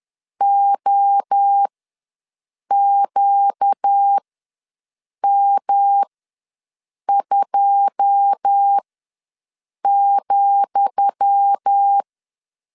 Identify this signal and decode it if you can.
Morse code, heard as a keyed tone, about 10 wpm, spaced as for 6 wpm, 800 Hz. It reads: OQM2,
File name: LEZIONI / 10 SIGNS / 10D COMA ws 10wpm 4314